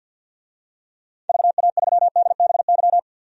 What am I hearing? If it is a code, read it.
Morse code, heard as a keyed tone, about 45 wpm, 700 Hz. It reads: VA3DBY
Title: DXCC-100 Call Signs - 6 Characters - 50wpm